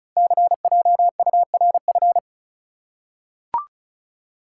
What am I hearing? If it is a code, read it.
Morse code, heard as a keyed tone, about 36 wpm, 700 Hz. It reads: CJURF
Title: Code Groups - 5 Letters - 35wpm